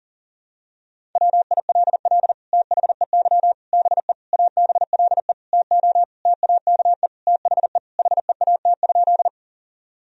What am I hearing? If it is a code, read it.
Morse code, heard as a keyed tone, about 40 wpm, 700 Hz. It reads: WILL THEY BE ABLE TO TAKE THE HEAT?
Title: Sentences from Top 1000 Words - Review Part 3 - 40wpm